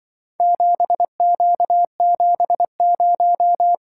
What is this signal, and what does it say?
Morse code, heard as a keyed tone, about 23 wpm, 700 Hz. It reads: 7Q70
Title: Speed-Racing - Four Letter-Number Combinations - 20wpm